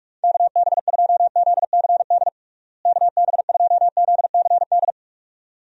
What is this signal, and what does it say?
Morse code, heard as a keyed tone, about 45 wpm, 700 Hz. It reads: KB2ZCD KB2ZCD
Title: USA Call Signs - 6 Characters - Repeated 2x - 45wpm